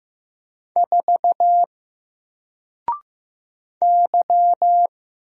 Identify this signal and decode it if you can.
Morse code, heard as a keyed tone, about 15 wpm, 700 Hz. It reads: 4 Y
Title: Single Letter-Number 15wpm